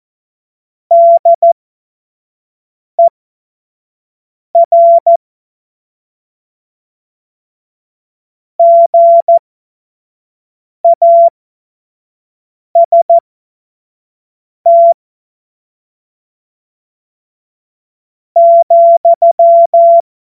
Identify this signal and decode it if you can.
Morse code, heard as a keyed tone, about 14 wpm, spaced as for 5 wpm, 680 Hz. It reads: DER GAST ,
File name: Morsekurs / Witze / Witz 04. Mai 2021, 25 F70